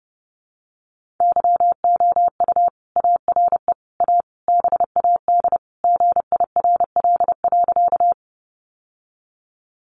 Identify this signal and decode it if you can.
Morse code, heard as a keyed tone, about 30 wpm, 700 Hz. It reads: YOU ARE A BAD GIRL.
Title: Sentences from Top 700 Words 30wpm